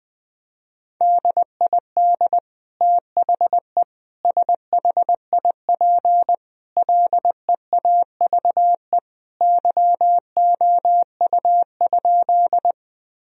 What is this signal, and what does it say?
Morse code, heard as a keyed tone, about 20 wpm, 700 Hz. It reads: DID THE SHIP LEAVE YOU?
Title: Sentences from Top 1000 Words - Review Part 2 - 20wpm